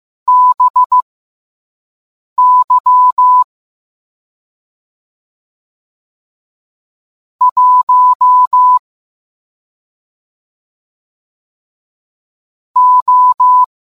Morse code, heard as a keyed tone, about 15 wpm, spaced as for 5 wpm, 1000 Hz. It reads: BY 1 O